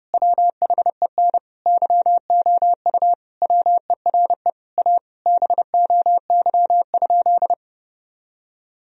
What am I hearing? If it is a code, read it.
Morse code, heard as a keyed tone, about 30 wpm, 700 Hz. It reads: WHEN YOU WERE A BOY?
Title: Sentences from Top 1000 Words - Review Part 1 - 30wpm